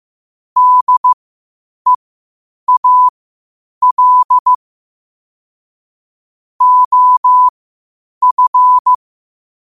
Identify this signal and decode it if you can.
Morse code, heard as a keyed tone, about 15 wpm, spaced as for 8 wpm, 1000 Hz. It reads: DEAL OF
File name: 8wpm